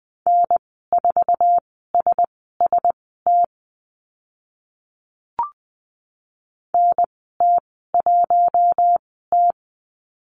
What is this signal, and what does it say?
Morse code, heard as a keyed tone, about 20 wpm, spaced as for 14 wpm, 700 Hz. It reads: N4SST NT1T
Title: 048 Lesson 9g - Number 4 - Call Signs 20wpm